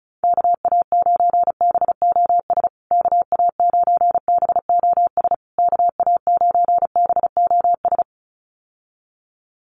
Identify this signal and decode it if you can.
Morse code, heard as a keyed tone, about 34 wpm, 700 Hz. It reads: KA9BOS KA9BOS KA9BOS
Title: USA Call Signs Repeated 3x 35wpm